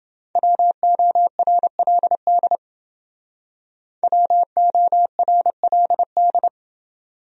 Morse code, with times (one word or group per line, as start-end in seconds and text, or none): 0.4-2.6 WORLD
4.0-6.5 WORLD